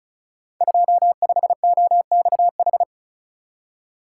Morse code, heard as a keyed tone, about 36 wpm, 700 Hz. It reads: 25OXH
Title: Code Groups - 5 Letter:Numbers - 35wpm